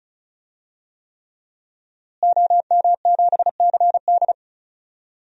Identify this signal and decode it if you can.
Morse code, heard as a keyed tone, about 36 wpm, 700 Hz. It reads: OM7CD
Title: DXCC-100 Call Signs - 5 Characters - Repeated 3x - 35wpm